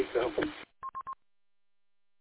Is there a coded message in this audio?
No, only speech.